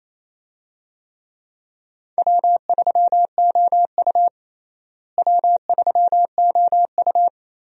Morse code, heard as a keyed tone, about 27 wpm, 700 Hz. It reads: W3OU W3OU
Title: USA Call Signs - 4 Characters - Repeated 2x - 28wpm